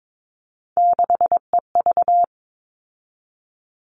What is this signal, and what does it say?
Morse code, heard as a keyed tone, about 22 wpm, 700 Hz. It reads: 6EV
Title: All Cut Numbers - 3-Digit 22wpm